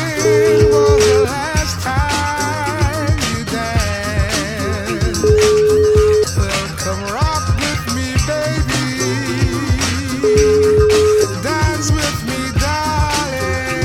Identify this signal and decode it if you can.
Ringback tone (Europe, 425 Hz), with noise.